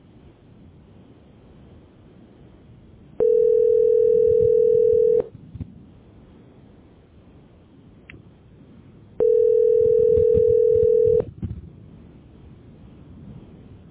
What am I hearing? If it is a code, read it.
Ringback tone (North America), 440 + 480 Hz.